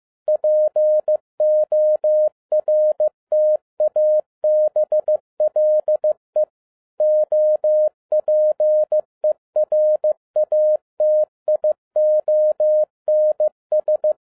Morse code, heard as a keyed tone, about 15 wpm, 610 Hz. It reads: PORTABLE OPERATIONS